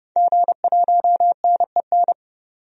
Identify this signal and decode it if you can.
Morse code, heard as a keyed tone, about 32 wpm, 690 Hz. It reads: G1NEN
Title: DXCC-100 Call Signs - 5 Characters - 30wpm